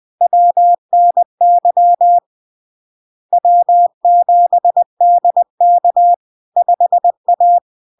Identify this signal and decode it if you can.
Morse code, heard as a keyed tone, about 20 wpm, 700 Hz. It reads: WNY W7DK 5A
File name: LARK-FD-2022-Log-in-CW-20wpm-700-Hz